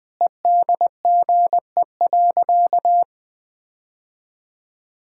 Morse code, heard as a keyed tone, about 20 wpm, 700 Hz. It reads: EDGE.
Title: Sentences with 3-Word Phrases - 20wpm